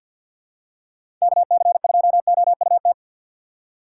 Morse code, heard as a keyed tone, about 49 wpm, 700 Hz. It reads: KC2OUT
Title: USA Call Signs - 6 Characters - 3x Character Spacing - 50wpm